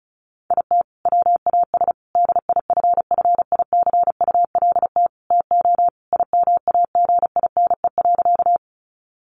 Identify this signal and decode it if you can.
Morse code, heard as a keyed tone, about 34 wpm, 700 Hz. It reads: IT WAS DIFFICULT TO IMAGINE.
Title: Sentences from Top 1400 Words 35wpm